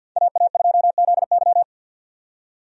Morse code, heard as a keyed tone, about 50 wpm, 700 Hz. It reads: AA2ZY
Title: USA Call Signs - 5 Characters - 50wpm